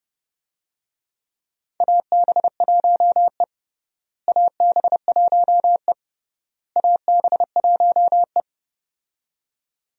Morse code, heard as a keyed tone, about 30 wpm, 700 Hz. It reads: AB1E AB1E AB1E
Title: USA Call Signs - 4 Characters - Repeated 3x - 30wpm